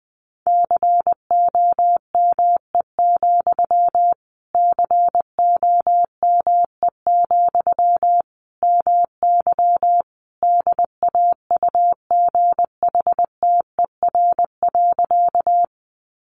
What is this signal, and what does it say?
Morse code, heard as a keyed tone, about 20 wpm, 700 Hz. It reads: COME, COME, MY DAUGHTER.
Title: Sentences from Top 1500 Words 20wpm